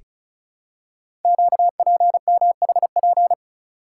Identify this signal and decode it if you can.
Morse code, heard as a keyed tone, about 36 wpm, 690 Hz. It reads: QPMHP
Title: Code Groups - 5 Letters - 35wpm